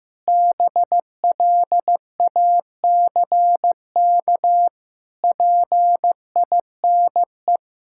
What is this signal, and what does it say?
Morse code, heard as a keyed tone, about 15 wpm, 700 Hz. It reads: BLACK PINE
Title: Pangram Challenge v2 - 15wpm